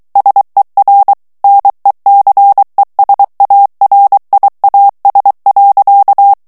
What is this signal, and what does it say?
Morse code, heard as a keyed tone, about 23 wpm, 800 Hz. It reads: SER NECESARIAS.
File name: morse mp3 / ESPAGNOL TEXTE EN CLAIR 2 VITESSE 22 MOTS MINUTES